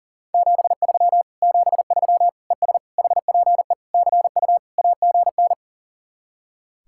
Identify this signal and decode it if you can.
Morse code, heard as a keyed tone, about 52 wpm, 700 Hz. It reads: 73 73 ES HPE CU AGN